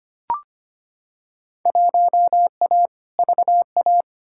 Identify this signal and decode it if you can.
Morse code, heard as a keyed tone, about 25 wpm, 700 Hz. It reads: 1A VA
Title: Field Day Exchanges Encore - 25wpm